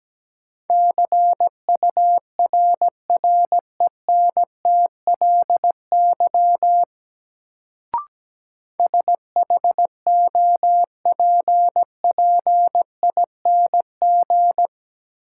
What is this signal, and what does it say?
Morse code, heard as a keyed tone, about 17 wpm, 700 Hz. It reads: CURRENTLY SHOPPING
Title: Sentences from Top 1000 Words - Review Part 7 - New Word Drill - 17wpm